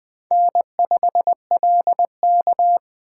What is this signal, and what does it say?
Morse code, heard as a keyed tone, about 23 wpm, 700 Hz. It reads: N5LK
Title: Speed Racing - USA 1x2 Call Signs 20wpm